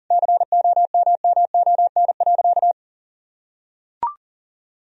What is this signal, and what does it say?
Morse code, heard as a keyed tone, about 40 wpm, 700 Hz. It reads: COMMON.
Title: Sentences from Top 1000 Words - Review Part 3 - 40wpm